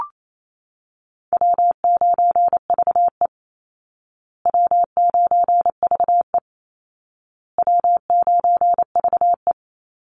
Morse code, heard as a keyed tone, about 28 wpm, 700 Hz. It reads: W9VE W9VE W9VE
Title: Call Signs Repeated 2x - 28wpm